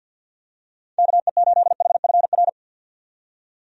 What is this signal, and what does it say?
Morse code, heard as a keyed tone, about 47 wpm, 700 Hz. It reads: KE8HFR
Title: USA Call Signs - 6 Characters - 50wpm